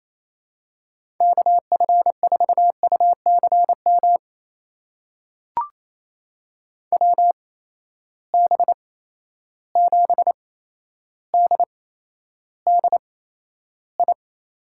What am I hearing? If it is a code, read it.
Morse code, heard as a keyed tone, about 28 wpm, spaced as for 12 wpm, 700 Hz. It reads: KF4UCM WB7DDI
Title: USA Call Signs - 6 Characters - 8x Character Spacing - 28wpm